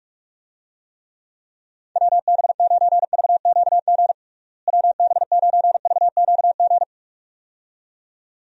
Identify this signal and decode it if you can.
Morse code, heard as a keyed tone, about 41 wpm, 700 Hz. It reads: WB9VQG WB9VQG
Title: USA Call Signs - 6 Characters - Repeated 2x - 45wpm